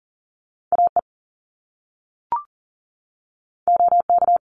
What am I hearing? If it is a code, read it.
Morse code, heard as a keyed tone, about 40 wpm, 700 Hz. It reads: AE OK
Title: Top Two Letter Combinations in Words 40wpm